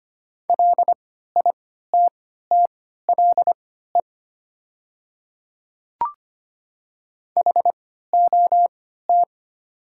Morse code, heard as a keyed tone, about 25 wpm, spaced as for 14 wpm, 700 Hz. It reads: LITTLE HOT
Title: 3x Character Spacing - Top 500 Words - 25wpm